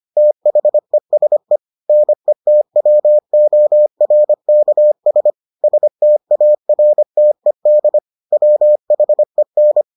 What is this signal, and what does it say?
Morse code, heard as a keyed tone, about 25 wpm, 600 Hz. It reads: THESE NETWORKS STARTED WHEN